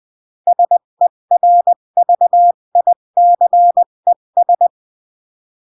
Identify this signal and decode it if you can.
Morse code, heard as a keyed tone, about 20 wpm, 700 Hz. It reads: SERVICES
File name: CW / jodorowsky / 82 commandments 20-15